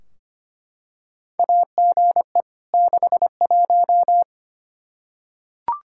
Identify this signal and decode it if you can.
Morse code, heard as a keyed tone, about 25 wpm, 700 Hz. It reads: AGE 61
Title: Practice QSO - Exchange 3 - More Info - 25wpm